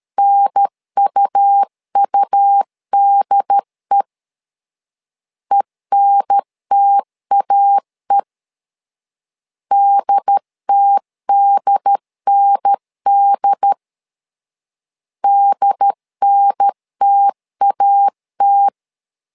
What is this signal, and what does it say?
Morse code, heard as a keyed tone, about 13 wpm, 800 Hz. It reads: NUUDE ENTAE DTDND DNTAT
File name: LEZIONI / 2 UHDN / 2 UHDN 12wpm 1456